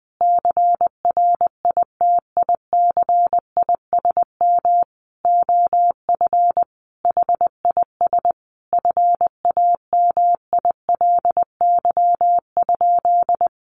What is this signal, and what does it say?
Morse code, heard as a keyed tone, about 20 wpm, 700 Hz. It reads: CRITICISM OF HIS FAMILY?
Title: Sentences from Top 1300 Words 20wpm